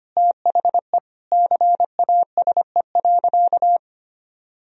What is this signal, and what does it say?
Morse code, heard as a keyed tone, about 25 wpm, 700 Hz. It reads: THE CASE.
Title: Sentences from Top 1000 Words - Review Part 3 - 25wpm